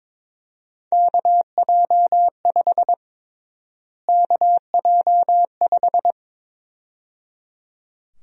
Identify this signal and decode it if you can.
Morse code, heard as a keyed tone, about 22 wpm, 700 Hz. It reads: KJ5 KJ5
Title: USA Call Sign Prefixes - Repeated 2x - 22wpm